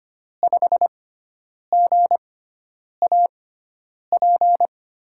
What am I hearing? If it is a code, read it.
Morse code, heard as a keyed tone, about 25 wpm, spaced as for 9 wpm, 700 Hz. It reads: 5GAP
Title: USA Call Signs - 5 Characters - 6x Character Spacing - 25wpm